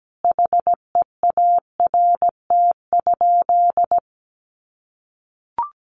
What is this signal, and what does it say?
Morse code, heard as a keyed tone, about 17 wpm, 700 Hz. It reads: HEART?
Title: Sentences from Top 1200 Words 17wpm